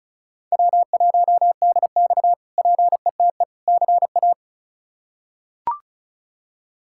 Morse code, heard as a keyed tone, about 36 wpm, 700 Hz. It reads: W1DX PETE CA
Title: K1USN SST QSO Practice - 35wpm